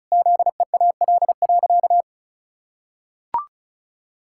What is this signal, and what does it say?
Morse code, heard as a keyed tone, about 34 wpm, 700 Hz. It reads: ZEAL.
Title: Pangram Challenge v2 - 35wpm